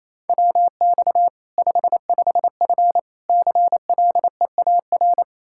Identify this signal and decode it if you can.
Morse code, heard as a keyed tone, about 28 wpm, 700 Hz. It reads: WX 55F CLEAR
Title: Practice QSO - Exchange 2 - Info - 28wpm